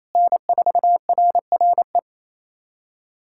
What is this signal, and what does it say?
Morse code, heard as a keyed tone, about 28 wpm, 700 Hz. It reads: N4RRE
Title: USA Callsign DE Callsign - 5 Characters - 28wpm